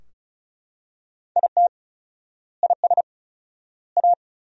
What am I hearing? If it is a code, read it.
Morse code, heard as a keyed tone, about 36 wpm, 700 Hz. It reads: IT IS A
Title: 4x Word-Spacing - Sentences from Top 400 Words - 35wpm